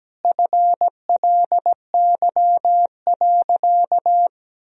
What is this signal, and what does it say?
Morse code, heard as a keyed tone, about 17 wpm, 700 Hz. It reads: FLY.
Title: Sentences from Top 1000 Words - Review Part 2 - 17wpm